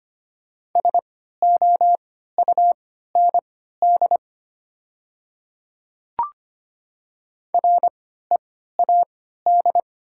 Morse code, heard as a keyed tone, about 25 wpm, spaced as for 14 wpm, 700 Hz. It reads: SOUND READ
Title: 3x Character Spacing - Top 200 Words - 25wpm